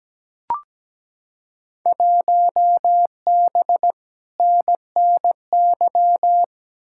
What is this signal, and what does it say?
Morse code, heard as a keyed tone, about 17 wpm, 700 Hz. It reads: 1B NNY
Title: Field Day Exchanges Encore - 17wpm